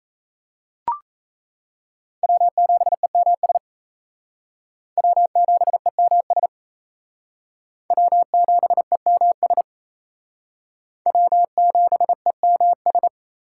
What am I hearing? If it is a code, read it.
Morse code, heard as a keyed tone, about 35 wpm, 700 Hz. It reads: W7EMS W7EMS W7EMS W7EMS
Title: Speed Racing - USA 1x3 Call Signs 28wpm